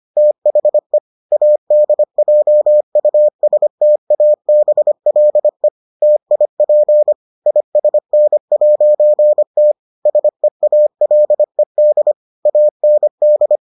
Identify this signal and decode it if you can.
Morse code, heard as a keyed tone, about 25 wpm, 600 Hz. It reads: THE ADJUSTABLE TIP ISN'T SEALED AND